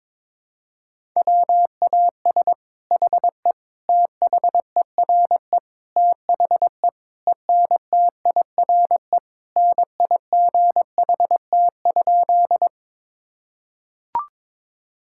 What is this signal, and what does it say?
Morse code, heard as a keyed tone, about 22 wpm, 700 Hz. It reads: WAS HE THERE THE ENTIRE NIGHT?
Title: Sentences from Top 1000 Words - Review Part 5 - 22wpm